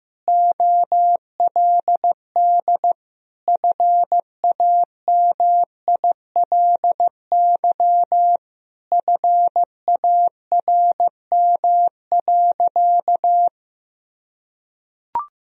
Morse code, heard as a keyed tone, about 15 wpm, 700 Hz. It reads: OLD FAMILY FARM.
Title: Sentences from Top 1000 Words - Review Part 2 - 15wpm